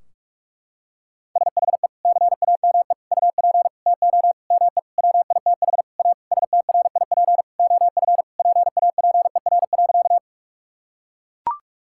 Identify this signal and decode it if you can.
Morse code, heard as a keyed tone, about 41 wpm, 700 Hz. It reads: SHE CAME UP TO ME WITH A STRIP OF PAPER.
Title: Sentences from Top 1600 Words 45wpm